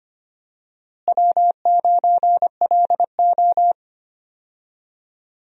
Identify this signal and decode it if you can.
Morse code, heard as a keyed tone, about 25 wpm, 700 Hz. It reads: W9LO
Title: USA Call Signs - 4 Characters - Repeated 2x - 25wpm